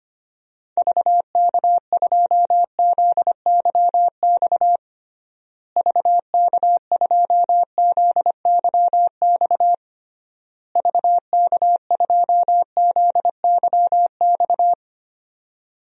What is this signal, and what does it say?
Morse code, heard as a keyed tone, about 25 wpm, 700 Hz. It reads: VK2ZYX VK2ZYX VK2ZYX
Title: DXCC-100 Call Signs - 6 Characters - Repeated 3x - 25wpm